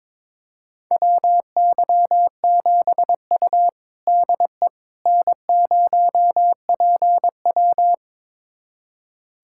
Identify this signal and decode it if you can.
Morse code, heard as a keyed tone, about 21 wpm, 700 Hz. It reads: WY7U DE N0PW
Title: USA Callsign DE Callsign - 4 Characters - 22wpm